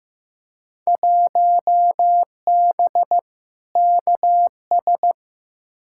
Morse code, heard as a keyed tone, about 15 wpm, 700 Hz. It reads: 1B KS
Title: Field Day Exchanges Encore - 15wpm